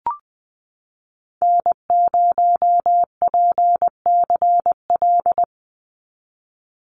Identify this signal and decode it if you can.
Morse code, heard as a keyed tone, about 20 wpm, 700 Hz. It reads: N0PCL
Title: NA SOTA Activator Callsigns 20wpm